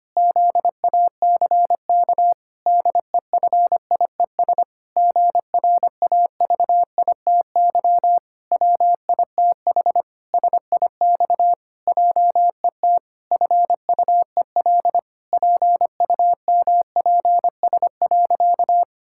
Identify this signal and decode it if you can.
Morse code, heard as a keyed tone, about 25 wpm, 700 Hz. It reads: ZACK DEFIES GRAVITY WITH SIX JET FUEL PUMPS.
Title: Pangram Challenge v2 - 25wpm